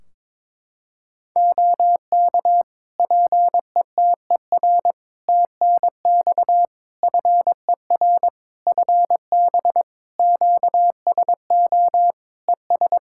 Morse code, heard as a keyed tone, about 22 wpm, 700 Hz. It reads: OK PETER TNX FER FB QSO ES
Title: K1USN SST QSO Practice 22wpm